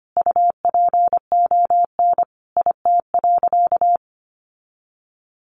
Morse code, heard as a keyed tone, about 25 wpm, 700 Hz. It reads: UPON IT.
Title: Sentences from Top 1300 Words 25wpm